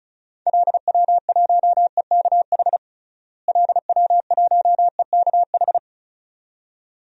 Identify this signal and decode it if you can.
Morse code, heard as a keyed tone, about 36 wpm, 700 Hz. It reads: LW1EKH LW1EKH
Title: DXCC-100 Call Signs - 6 Characters - Repeated 2x - 35wpm